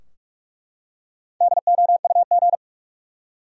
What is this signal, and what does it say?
Morse code, heard as a keyed tone, about 43 wpm, 700 Hz. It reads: DOUG
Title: USA Common Names - Max 5 Letters - 45wpm